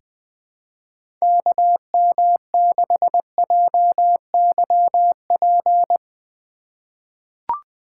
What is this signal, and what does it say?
Morse code, heard as a keyed tone, about 20 wpm, 700 Hz. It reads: KM6JYP
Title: USA Call Signs - 4x Character Spacing - 20wpm